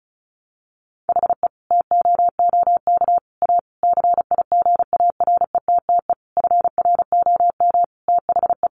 Morse code, heard as a keyed tone, about 34 wpm, 700 Hz. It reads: HE TOOK A CIGARETTE FROM THE
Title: Sentences from Top 1000 Words 35wpm